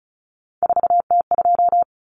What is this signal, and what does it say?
Morse code, heard as a keyed tone, about 34 wpm, 700 Hz. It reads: 4T2
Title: Common Cut Numbers - 3-Digit 35wpm